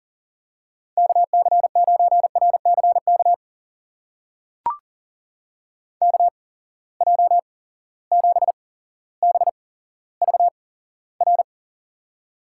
Morse code, heard as a keyed tone, about 38 wpm, spaced as for 17 wpm, 700 Hz. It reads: KC9RCK KJ7BVR
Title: USA Call Signs - 6 Characters - 8x Character Spacing - 40wpm